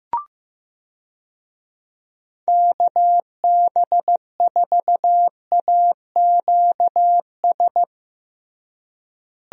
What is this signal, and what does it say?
Morse code, heard as a keyed tone, about 15 wpm, 700 Hz. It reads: KB4AQS
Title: USA Call Signs - 6 Characters - Repeated 2x - 15wpm